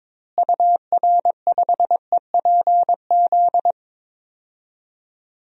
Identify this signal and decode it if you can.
Morse code, heard as a keyed tone, about 22 wpm, 700 Hz. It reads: UR5EPZ
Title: DXCC-100 Call Signs - 6 Characters - Repeated 2x - 22wpm